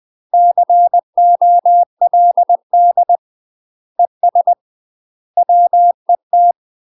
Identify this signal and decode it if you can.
Morse code, heard as a keyed tone, about 20 wpm, 700 Hz. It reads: COLD ES WET
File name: UK MASTER QSOs TRAINING/20WPM